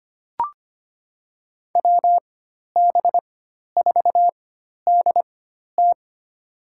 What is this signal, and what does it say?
Morse code, heard as a keyed tone, about 25 wpm, spaced as for 12 wpm, 700 Hz. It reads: WB4DT
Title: USA Call Signs - 5 Characters - 4x Character Spacing - 25wpm